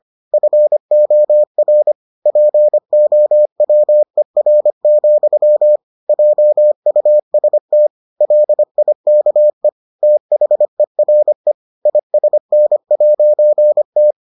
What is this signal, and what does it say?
Morse code, heard as a keyed tone, about 25 wpm, 600 Hz. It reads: FOR POWER, JUST LIKE THERE ISN'T